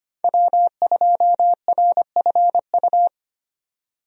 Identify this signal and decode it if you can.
Morse code, heard as a keyed tone, about 25 wpm, 700 Hz. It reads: W2RFU
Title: USA Call Signs - 5 Characters - 25wpm